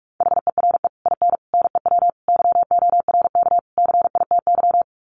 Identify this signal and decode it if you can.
Morse code, heard as a keyed tone, about 44 wpm, 700 Hz. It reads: HERE IN NEW YORK CITY
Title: Sets of 5 Words Encore2 45wpm